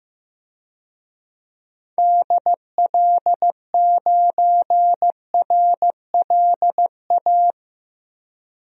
Morse code, heard as a keyed tone, about 15 wpm, 700 Hz. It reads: DL9RLA
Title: DXCC-100 Call Signs - 6 Characters - Repeated 3x - 15wpm